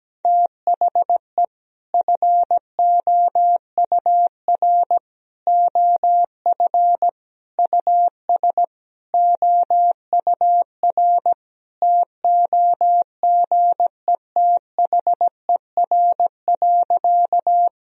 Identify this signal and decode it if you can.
Morse code, heard as a keyed tone, about 17 wpm, 700 Hz. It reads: THE FOUR OF US OUR TOGETHER.
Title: Sentences from Top 1000 Words - Review Part 1 - 17wpm